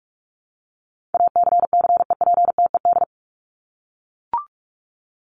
Morse code, heard as a keyed tone, about 44 wpm, 700 Hz. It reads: ACCEPTED
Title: ED Suffix - Top 100 Words with Base Words 45wpm